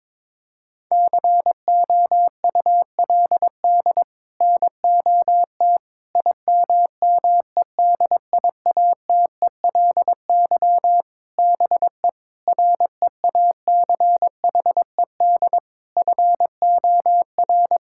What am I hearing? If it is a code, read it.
Morse code, heard as a keyed tone, about 21 wpm, 700 Hz. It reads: COULD NOT IMMEDIATELY BE REACHED FOR
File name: practice / 22wpm / Sets of 7 Words - 22wpm